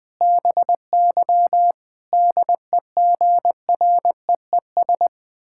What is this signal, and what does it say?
Morse code, heard as a keyed tone, about 20 wpm, 700 Hz. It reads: BY DEGREES
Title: Pangram Challenge v2 - 20wpm